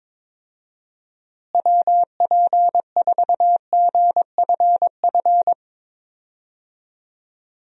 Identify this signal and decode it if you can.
Morse code, heard as a keyed tone, about 22 wpm, 700 Hz. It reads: WP4GFF
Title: USA Call Signs - 6 Characters - Repeated 3x - 22wpm